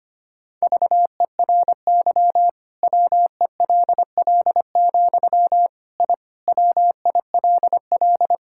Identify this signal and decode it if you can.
Morse code, heard as a keyed tone, about 25 wpm, 700 Hz. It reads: VERY WELL, I WILL
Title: Sentences from Top 1000 Words - Review Part 5 - 25wpm